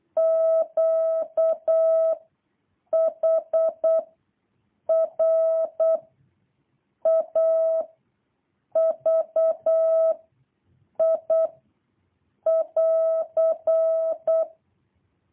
Morse code, heard as a keyed tone, about 8 wpm, 650 Hz. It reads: Q H R A V I <AR>
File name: jh8cbh / 2023cw